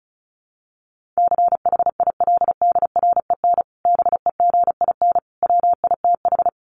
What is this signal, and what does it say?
Morse code, heard as a keyed tone, about 34 wpm, 700 Hz. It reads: CHILDREN BEGIN WITH
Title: Sentences from Top 1500 Words 35wpm